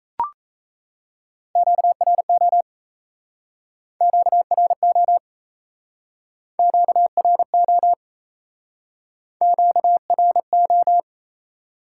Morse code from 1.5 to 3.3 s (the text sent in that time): QRO